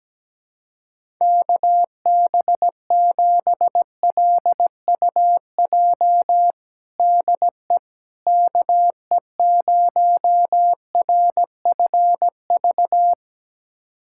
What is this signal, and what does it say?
Morse code, heard as a keyed tone, about 17 wpm, 700 Hz. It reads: KB7LUJ DE KE0RFV
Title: USA Callsign DE Callsign - 6 Characters - 17wpm